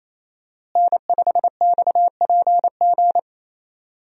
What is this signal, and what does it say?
Morse code, heard as a keyed tone, about 28 wpm, 700 Hz. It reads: N5XPG
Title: USA Call Signs - 5 Characters - Repeated 3x - 28wpm